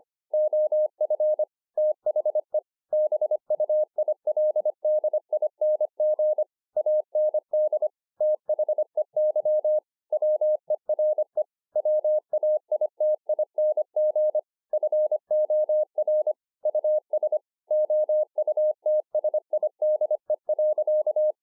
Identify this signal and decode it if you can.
Morse code, heard as a keyed tone, about 25 wpm, 610 Hz. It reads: OF THE BUILDING AND THEY WERE WAITING FOR US OUTSIDE.
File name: 2011-04-mp3 / THE 3.11 3 25WPM